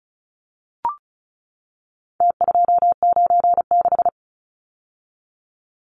Morse code, heard as a keyed tone, about 34 wpm, 700 Hz. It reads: T296
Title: Numbers - Contest Style - 4-Digit 35wpm